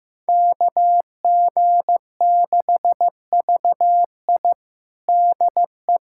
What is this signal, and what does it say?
Morse code, heard as a keyed tone, about 15 wpm, 700 Hz. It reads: KG6VI DE
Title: USA Callsign DE Callsign - 5 Characters - 15wpm